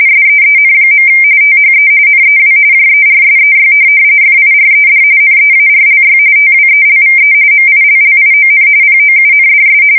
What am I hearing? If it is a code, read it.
RTTY, about 45.45 baud. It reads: SEEN. THE NEW JERSEY FIRED NEARLY TWELVE MILLION POUNDS
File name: fsk170 / USS-New-Jersey